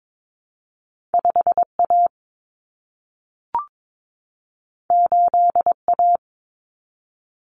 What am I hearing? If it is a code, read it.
Morse code, heard as a keyed tone, about 22 wpm, 700 Hz. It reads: 5A 8A
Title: Common Cut Numbers - 2-Digit 22wpm